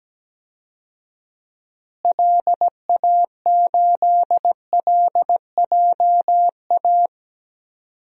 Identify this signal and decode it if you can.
Morse code, heard as a keyed tone, about 17 wpm, 700 Hz. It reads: LA8LJA
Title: DXCC-100 Call Signs - 6 Characters - Repeated 2x - 17wpm